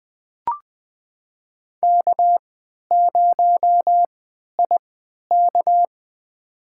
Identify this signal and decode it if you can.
Morse code, heard as a keyed tone, about 20 wpm, spaced as for 11 wpm, 700 Hz. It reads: K0IK
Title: USA Call Signs - 3x Character Spacing - 20wpm